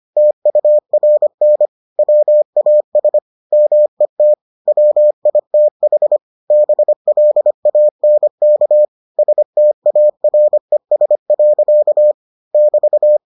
Morse code, heard as a keyed tone, about 25 wpm, 600 Hz. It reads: TURN WAS MET WITH BLANK STARES. <BT>